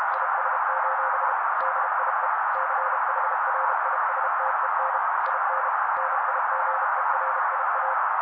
Morse code, heard as a keyed tone, about 32 wpm, 530 Hz. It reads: IS: DIE CHRISTEN IN NIGERIA
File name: daily / 31wpm noise